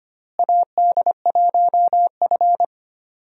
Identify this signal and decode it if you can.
Morse code, heard as a keyed tone, about 25 wpm, 700 Hz. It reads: AD1F
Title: USA Call Signs - 4 Characters - Repeated 3x - 25wpm